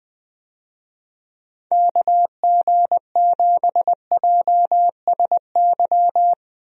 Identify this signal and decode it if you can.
Morse code, heard as a keyed tone, about 20 wpm, 700 Hz. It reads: KG7JSY
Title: USA Call Signs - 6 Characters - Repeated 2x - 20wpm